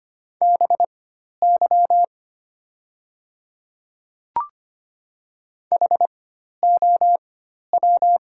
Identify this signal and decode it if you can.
Morse code, heard as a keyed tone, about 25 wpm, spaced as for 12 wpm, 690 Hz. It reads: BY HOW
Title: 4x Character Spacing - Top 100 Words - 25wpm